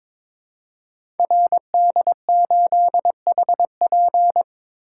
Morse code, heard as a keyed tone, about 22 wpm, 700 Hz. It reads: RD8HP
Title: Code Groups - 5 Letter:Numbers - 22wpm